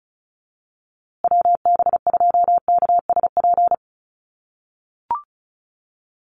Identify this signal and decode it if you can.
Morse code, heard as a keyed tone, about 34 wpm, 700 Hz. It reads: WB2KSP
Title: USA Call Signs Repeated 3x 35wpm